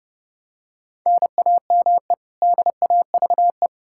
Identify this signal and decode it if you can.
Morse code, heard as a keyed tone, about 32 wpm, 700 Hz. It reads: NAME DAVE
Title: Practice QSO - Exchange 1 - Beginning - 30wpm